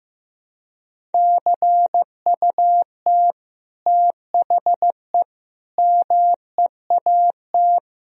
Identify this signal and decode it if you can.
Morse code, heard as a keyed tone, about 16 wpm, 700 Hz. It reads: CUT THE MEAT
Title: Sentences from Top 1000 Words - Review Part 3 - 15wpm